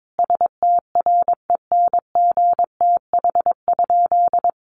Morse code, heard as a keyed tone, about 22 wpm, 700 Hz. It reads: STRENGTH?
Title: Sentences from Top 1300 Words 22wpm